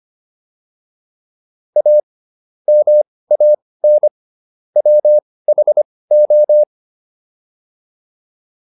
Morse code, heard as a keyed tone, about 25 wpm, spaced as for 18 wpm, 600 Hz. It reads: A MAN WHO